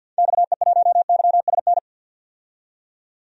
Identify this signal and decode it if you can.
Morse code, heard as a keyed tone, about 52 wpm, 700 Hz. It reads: XE1YSN
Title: DXCC-100 Call Signs - 6 Characters - 50wpm